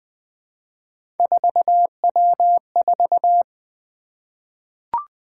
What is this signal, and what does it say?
Morse code, heard as a keyed tone, about 20 wpm, 700 Hz. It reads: 4W4
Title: Three Letter-Number Combinations - 20wpm